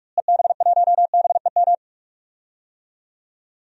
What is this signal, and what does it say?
Morse code, heard as a keyed tone, about 41 wpm, 700 Hz. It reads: EB1BEM
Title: DXCC-100 Call Signs - 6 Characters - Repeated 3x - 45wpm